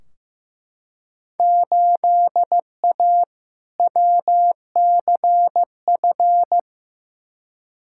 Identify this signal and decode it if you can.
Morse code, heard as a keyed tone, about 15 wpm, 700 Hz. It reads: 8A WCF
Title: Field Day Exchanges Encore - 15wpm